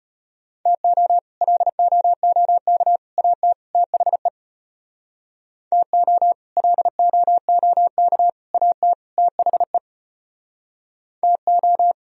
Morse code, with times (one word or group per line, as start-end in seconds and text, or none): 0.7-1.2 TO
1.4-3.0 LOOK
3.2-3.5 AT
3.8-4.3 THE
5.7-6.3 TO
6.6-8.3 LOOK
8.5-8.9 AT
9.2-9.8 THE
11.2-11.9 TO